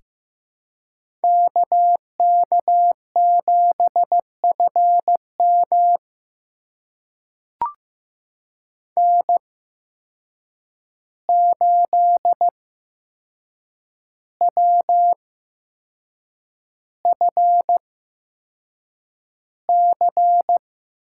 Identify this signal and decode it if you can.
Morse code, heard as a keyed tone, about 15 wpm, spaced as for 6 wpm, 700 Hz. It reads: KK7FM N8WFC
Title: USA Call Signs - 5 Characters - 8x Character Spacing - 15wpm